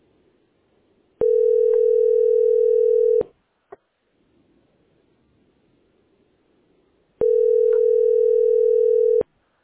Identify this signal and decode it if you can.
Ringback tone (North America), 440 + 480 Hz.